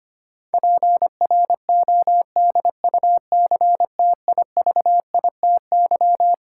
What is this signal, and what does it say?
Morse code, heard as a keyed tone, about 25 wpm, 700 Hz. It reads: PRODUCTIVITY
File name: Twelve Letter Words - 25wpm